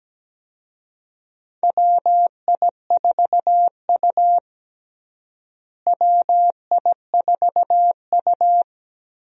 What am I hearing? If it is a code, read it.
Morse code, heard as a keyed tone, about 17 wpm, 700 Hz. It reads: WI4U WI4U
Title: USA Call Signs - 4 Characters - Repeated 3x - 17wpm